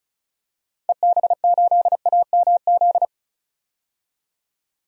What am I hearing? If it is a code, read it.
Morse code, heard as a keyed tone, about 34 wpm, 700 Hz. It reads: EB8AMZ
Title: DXCC-100 Call Signs - 6 Characters - 35wpm